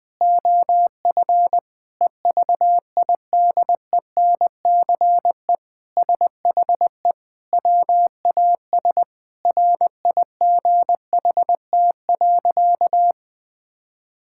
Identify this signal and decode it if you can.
Morse code, heard as a keyed tone, about 20 wpm, 700 Hz. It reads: OF EVIDENCE SHE WAS RIGHT.
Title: Sentences with 4-Word Phrases - 20wpm